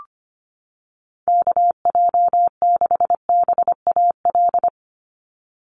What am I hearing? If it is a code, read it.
Morse code, heard as a keyed tone, about 25 wpm, 700 Hz. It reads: KJ6BAL
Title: USA Call Signs 25wpm